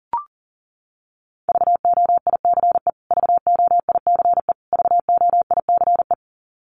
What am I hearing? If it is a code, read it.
Morse code, heard as a keyed tone, about 40 wpm, 700 Hz. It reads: VOICE VOICE VOICE
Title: Top 500 Words Repeated 40wpm